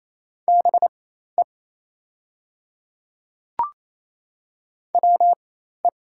Morse code, heard as a keyed tone, about 28 wpm, spaced as for 13 wpm, 700 Hz. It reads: BE WE
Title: 4x Character Spacing - Top 100 Words - 28wpm